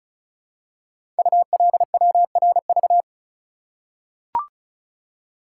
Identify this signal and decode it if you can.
Morse code, heard as a keyed tone, about 36 wpm, 700 Hz. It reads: ULWRV
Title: Code Groups - 5 Letters - 35wpm